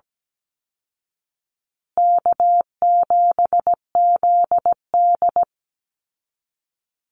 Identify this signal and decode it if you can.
Morse code, heard as a keyed tone, about 17 wpm, 700 Hz. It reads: K7ZD
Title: Easy Call Signs Repeated 3x - 17wpm